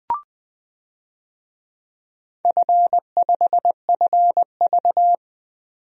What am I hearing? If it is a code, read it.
Morse code, heard as a keyed tone, about 20 wpm, 700 Hz. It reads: F5FV
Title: DXCC-100 Call Signs - 4 Characters - 20wpm